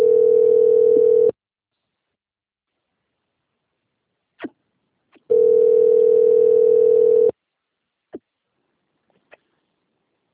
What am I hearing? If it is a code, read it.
Ringback tone (North America), 440 + 480 Hz.
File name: rg-1600-unknown- - - .5986